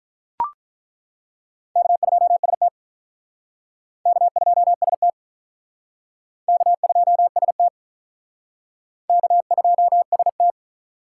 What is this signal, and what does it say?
Morse code, heard as a keyed tone, about 43 wpm, 700 Hz. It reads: K2ST K2ST K2ST K2ST
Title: Speed Racing - USA Call Signs 35wpm